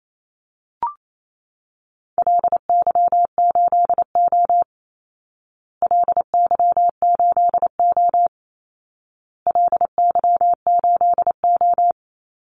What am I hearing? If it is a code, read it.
Morse code, heard as a keyed tone, about 28 wpm, 700 Hz. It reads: LY8O LY8O LY8O
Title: Call Signs Repeated 2x - 28wpm